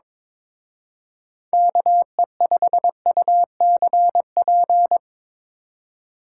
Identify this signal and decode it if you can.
Morse code, heard as a keyed tone, about 25 wpm, 700 Hz. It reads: KE5UCP
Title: Speed Racing - USA Call Signs 22wpm